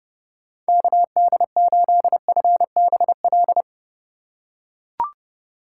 Morse code, heard as a keyed tone, about 31 wpm, 700 Hz. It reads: KD8FBL
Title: USA Call Signs - 6 Characters - 4x Character Spacing - 30wpm